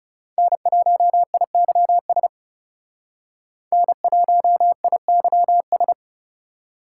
Morse code, heard as a keyed tone, about 38 wpm, 700 Hz. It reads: N1IYS N1IYS